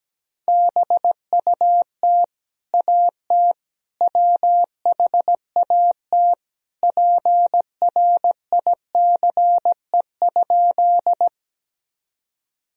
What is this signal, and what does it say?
Morse code, heard as a keyed tone, about 17 wpm, 700 Hz. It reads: BUT AT WHAT PRICE?
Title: Sentences from Top 1000 Words - Review Part 4 - 17wpm